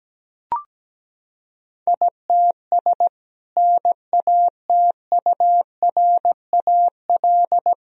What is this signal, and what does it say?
Morse code, heard as a keyed tone, about 17 wpm, 700 Hz. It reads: ITS NATURAL
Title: Sentences from Top 1000 Words - Review Part 4 - 17wpm